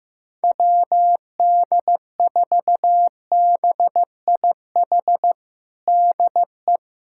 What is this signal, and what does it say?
Morse code, heard as a keyed tone, about 16 wpm, 700 Hz. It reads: WD4BIH DE
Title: USA Callsign DE Callsign 15wpm